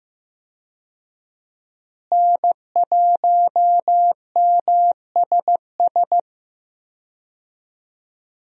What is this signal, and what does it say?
Morse code, heard as a keyed tone, about 15 wpm, 700 Hz. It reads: N1MSS
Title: USA Call Signs - 5 Characters - Repeated 2x - 15wpm